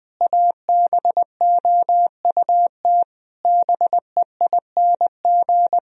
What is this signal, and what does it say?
Morse code, heard as a keyed tone, about 20 wpm, 700 Hz. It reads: ABOUT BEING
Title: Sentences from Top 1000 Words - Review Part 4 - 20wpm